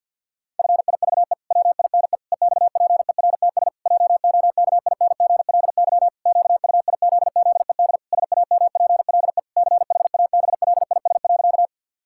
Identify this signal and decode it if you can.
Morse code, heard as a keyed tone, about 52 wpm, 700 Hz. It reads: FIVE WINE EXPERTS JOKINGLY QUIZZED SAMPLE CHABLIS.
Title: Pangram Challenge v2 - 50wpm